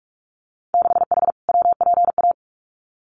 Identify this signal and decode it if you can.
Morse code, heard as a keyed tone, about 46 wpm, 700 Hz. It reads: 6H WPA
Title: Winter Field Day Exchanges Encore 45wpm